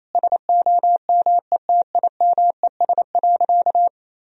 Morse code, heard as a keyed tone, about 27 wpm, 700 Hz. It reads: SOMETIMES.
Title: Sentences from Top 1000 Words - Review Part 3 - 28wpm